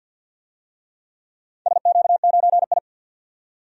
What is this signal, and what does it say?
Morse code, heard as a keyed tone, about 50 wpm, 700 Hz. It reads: SQ9I
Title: DXCC-100 Call Signs - 4 Characters - 50wpm